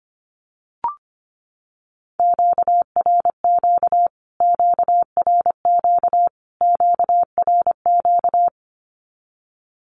Morse code, heard as a keyed tone, about 25 wpm, 700 Hz. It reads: QRQ QRQ QRQ
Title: Q-Codes Repeated 3x 25wpm